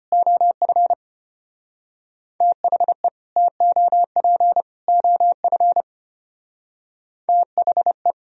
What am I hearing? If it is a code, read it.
Morse code, heard as a keyed tone, about 31 wpm, 700 Hz. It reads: OF THE TOP OF THE
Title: Speed-Racing - Sets of 3 Words 25wpm